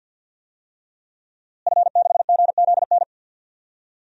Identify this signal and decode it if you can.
Morse code, heard as a keyed tone, about 52 wpm, 700 Hz. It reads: F6GZN
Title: DXCC-100 Call Signs - 5 Characters - Repeated 3x - 50wpm